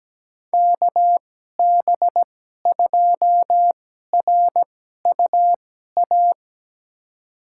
Morse code, heard as a keyed tone, about 17 wpm, spaced as for 12 wpm, 700 Hz. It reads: KB2RUA
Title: USA Call Signs - 2x Character Spacing - 17wpm